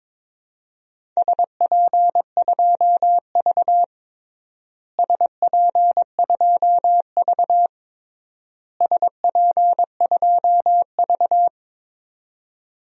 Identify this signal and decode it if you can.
Morse code, heard as a keyed tone, about 22 wpm, 700 Hz. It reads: SP2V SP2V SP2V
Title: DXCC-100 Call Signs - 4 Characters - Repeated 3x - 22wpm